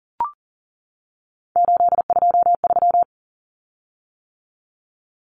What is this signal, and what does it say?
Morse code, heard as a keyed tone, about 40 wpm, 700 Hz. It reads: 823
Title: Common Cut Numbers - 3-Digit 40wpm